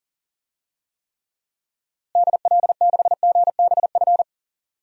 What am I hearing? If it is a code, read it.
Morse code, heard as a keyed tone, about 40 wpm, 700 Hz. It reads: DL6GBF
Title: DXCC-100 Call Signs - 6 Characters - Repeated 2x - 40wpm